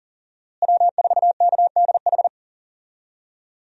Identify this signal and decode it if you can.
Morse code, heard as a keyed tone, about 40 wpm, 700 Hz. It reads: W4KDH
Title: USA Call Signs - 5 Characters - Repeated 2x - 40wpm